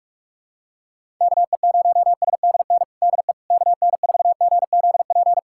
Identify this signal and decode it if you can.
Morse code, heard as a keyed tone, about 46 wpm, 700 Hz. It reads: KE0SDN DE KN4GZP
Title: USA Callsign DE Callsign - 6 Characters - 45wpm